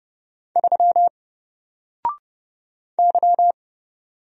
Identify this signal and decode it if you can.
Morse code, heard as a keyed tone, about 29 wpm, 700 Hz. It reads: 3 Y
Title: Single Letter-Number Alt 30wpm